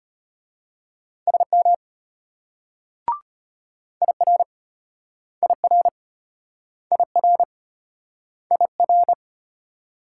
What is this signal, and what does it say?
Morse code, heard as a keyed tone, about 33 wpm, 700 Hz. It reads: SM IR IR IR IR